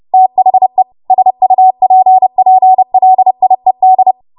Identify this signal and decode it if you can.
Morse code, heard as a keyed tone, about 30 wpm, 750 Hz. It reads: THE SUPPLIED
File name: Morse / 30WPM